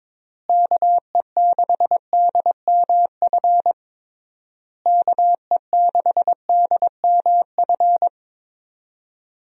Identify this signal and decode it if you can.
Morse code, heard as a keyed tone, about 22 wpm, 700 Hz. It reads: KE6DMF KE6DMF
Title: USA Call Signs - 6 Characters - Repeated 3x - 22wpm